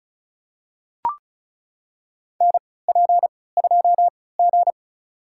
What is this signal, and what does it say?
Morse code, heard as a keyed tone, about 36 wpm, spaced as for 20 wpm, 700 Hz. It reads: NP2G
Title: USA Call Signs - 4 Characters - 3x Character Spacing - 35wpm